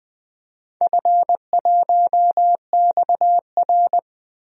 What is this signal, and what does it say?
Morse code, heard as a keyed tone, about 20 wpm, 700 Hz. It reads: F1XR
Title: DXCC-100 Call Signs - 4 Characters - 20wpm